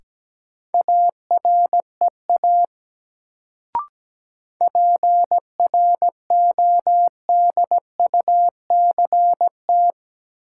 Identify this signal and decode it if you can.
Morse code, heard as a keyed tone, about 17 wpm, 700 Hz. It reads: AREA PRODUCT
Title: Sentences from Top 1000 Words - Review Part 2 - New Word Drill - 17wpm